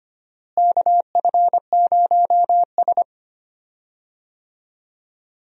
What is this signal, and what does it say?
Morse code, heard as a keyed tone, about 25 wpm, 700 Hz. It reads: KF0S
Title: USA Call Signs - 4 Characters - Repeated 3x - 25wpm